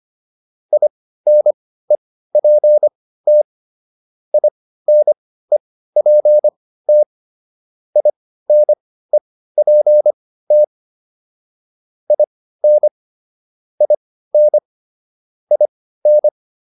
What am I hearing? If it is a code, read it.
Morse code, heard as a keyed tone, about 25 wpm, spaced as for 15 wpm, 600 Hz. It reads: INEPT INEPT INEPT IN IN IN
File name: IN5 15